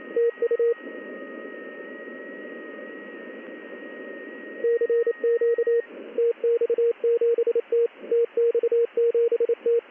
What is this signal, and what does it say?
Morse code, heard as a keyed tone, about 28 wpm, 470 Hz. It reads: TU CQ TX7T TX7T
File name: contests / ss-ssb-k5tr- -1700